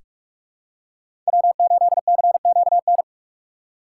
Morse code, heard as a keyed tone, about 45 wpm, 700 Hz. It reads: W8CQN
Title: USA Call Signs - 5 Characters - 6x Character Spacing - 45wpm